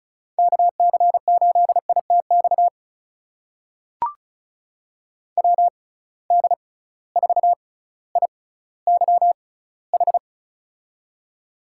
Morse code, heard as a keyed tone, about 36 wpm, spaced as for 18 wpm, 700 Hz. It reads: KC8ITX WD4IYH
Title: USA Call Signs - 6 Characters - 6x Character Spacing - 35wpm